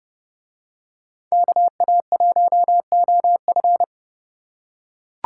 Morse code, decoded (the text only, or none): KA1OF